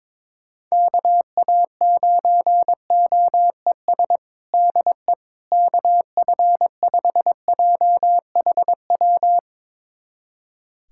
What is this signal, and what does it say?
Morse code, heard as a keyed tone, about 22 wpm, 700 Hz. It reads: KA9OES DE KF5JHW
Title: USA Callsign DE Callsign 22wpm